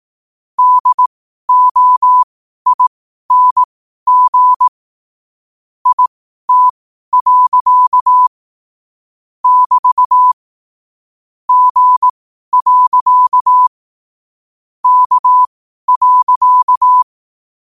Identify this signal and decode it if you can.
Morse code, heard as a keyed tone, about 18 wpm, spaced as for 12 wpm, 1000 Hz. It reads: DOING IT. <BT> G. K.
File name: qotdfiles / 12wpm